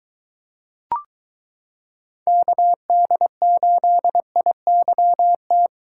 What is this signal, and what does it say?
Morse code, heard as a keyed tone, about 22 wpm, 700 Hz. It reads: KD8IYT